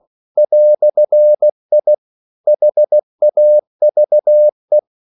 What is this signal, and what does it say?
Morse code, heard as a keyed tone, about 16 wpm, 600 Hz. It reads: "I HAVE
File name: CW / Practice / EnglishQuotations0216